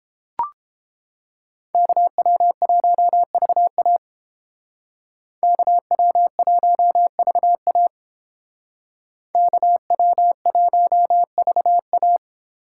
Morse code, heard as a keyed tone, about 28 wpm, 700 Hz. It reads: KW1VA KW1VA KW1VA